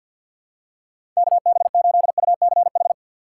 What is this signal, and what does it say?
Morse code, heard as a keyed tone, about 51 wpm, 700 Hz. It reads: KB8UCH
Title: USA Call Signs - 6 Characters - 4x Character Spacing - 50wpm